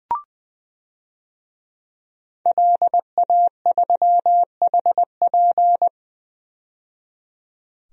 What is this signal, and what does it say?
Morse code, heard as a keyed tone, about 20 wpm, 700 Hz. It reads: LA3HP
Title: DXCC-100 Call Signs - 5 Characters - 20wpm